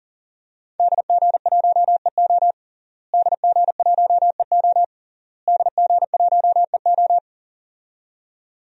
Morse code, heard as a keyed tone, about 37 wpm, 700 Hz. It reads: DG1EO DG1EO DG1EO
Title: DXCC-100 Call Signs - 5 Characters - Repeated 3x - 40wpm